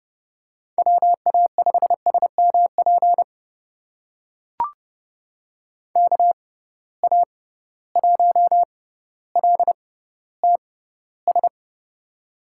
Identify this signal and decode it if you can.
Morse code, heard as a keyed tone, about 30 wpm, spaced as for 15 wpm, 700 Hz. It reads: WA5SMP KA1LTS
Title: USA Call Signs - 6 Characters - 6x Character Spacing - 30wpm